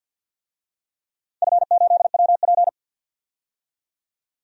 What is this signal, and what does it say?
Morse code, heard as a keyed tone, about 53 wpm, 700 Hz. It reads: F8WP
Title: DXCC-100 Call Signs - 4 Characters - 50wpm